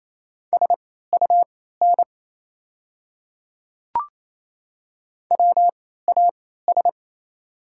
Morse code, heard as a keyed tone, about 28 wpm, spaced as for 16 wpm, 700 Hz. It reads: SUN WAS
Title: 3x Character Spacing - Top 300 Words - 28wpm